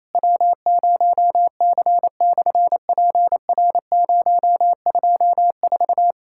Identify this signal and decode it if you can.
Morse code, heard as a keyed tone, about 28 wpm, 700 Hz. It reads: W0C/PR024
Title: SOTA QSO Practice 28wpm